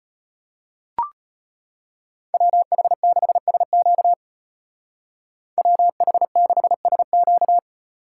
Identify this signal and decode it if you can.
Morse code, heard as a keyed tone, about 31 wpm, 700 Hz. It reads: WH6SQ WH6SQ
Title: Speed Racing - USA Call Signs 25wpm